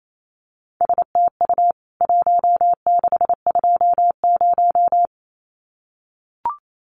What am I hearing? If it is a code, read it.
Morse code, heard as a keyed tone, about 28 wpm, 700 Hz. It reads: STU 1620
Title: CWT Exchange 28wpm